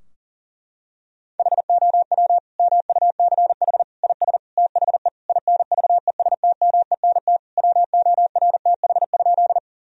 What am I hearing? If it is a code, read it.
Morse code, heard as a keyed tone, about 40 wpm, 700 Hz. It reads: HOW MUCH IS THE INVESTMENT WORTH?
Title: Sentences from Top 1000 Words - Review Part 4 - 40wpm